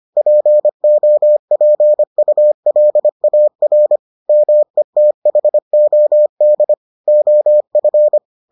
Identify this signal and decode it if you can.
Morse code, heard as a keyed tone, about 25 wpm, 600 Hz. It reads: POPULAR METHOD OF